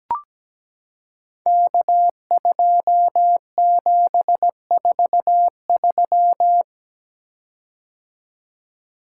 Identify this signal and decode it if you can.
Morse code, heard as a keyed tone, about 17 wpm, 700 Hz. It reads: K2743
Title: Code Groups - 5 Letter:Numbers - 17wpm